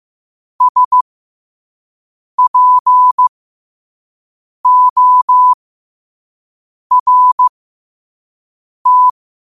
Morse code, heard as a keyed tone, about 15 wpm, spaced as for 5 wpm, 1000 Hz. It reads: SPORT
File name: qotdfiles / 5wpm